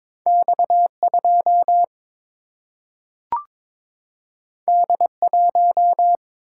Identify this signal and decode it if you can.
Morse code, heard as a keyed tone, about 22 wpm, 700 Hz. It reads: X2 D1
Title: Single-Letter Single-Number 22wpm